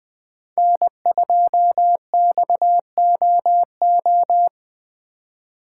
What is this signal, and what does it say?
Morse code, heard as a keyed tone, about 20 wpm, 700 Hz. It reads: N2XOO
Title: USA Call Signs - 5 Characters - 3x Character Spacing - 20wpm